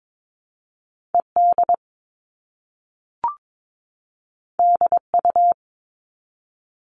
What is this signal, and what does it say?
Morse code, heard as a keyed tone, about 22 wpm, 700 Hz. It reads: ED DU
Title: All Cut Numbers - 2-Digit 22wpm